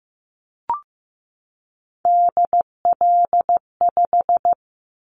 Morse code, heard as a keyed tone, about 15 wpm, 700 Hz. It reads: DL5
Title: Callsign Prefixes for Top 50 DXCC - 15wpm